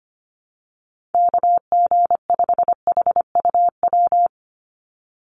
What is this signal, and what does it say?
Morse code, heard as a keyed tone, about 25 wpm, 690 Hz. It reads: KG5HUW
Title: USA Call Signs 25wpm